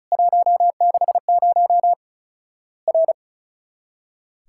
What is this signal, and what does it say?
Morse code, heard as a keyed tone, about 35 wpm, 700 Hz. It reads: 160 R
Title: Worked All Europe DX Contest - QTC Exchange - 35wpm